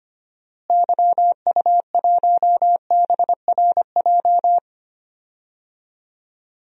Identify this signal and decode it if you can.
Morse code, heard as a keyed tone, about 25 wpm, 700 Hz. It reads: YU1BRJ
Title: DXCC-100 Call Signs - 6 Characters - Repeated 2x - 25wpm